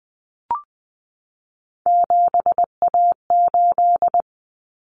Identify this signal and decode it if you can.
Morse code, heard as a keyed tone, about 20 wpm, 700 Hz. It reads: 7A8
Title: Common Cut Numbers - 3-Digit 20wpm